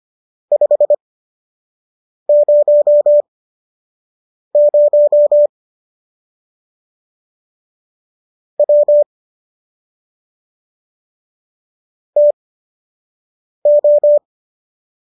Morse code, heard as a keyed tone, about 25 wpm, spaced as for 6 wpm, 590 Hz. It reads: 500 W TO
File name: qso012 06